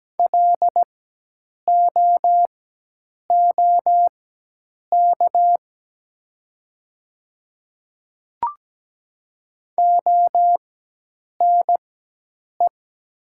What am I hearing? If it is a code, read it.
Morse code, heard as a keyed tone, about 17 wpm, spaced as for 8 wpm, 700 Hz. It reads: LOOK ONE
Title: 4x Character Spacing - Top 100 Words - 17wpm